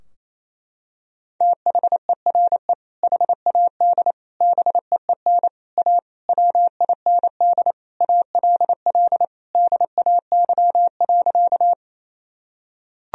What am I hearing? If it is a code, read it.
Morse code, heard as a keyed tone, about 28 wpm, 700 Hz. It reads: THERE HAD BEEN A WIND ALL DAY.
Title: Sentences from Top 1000 Words - Review Part 2 - 28wpm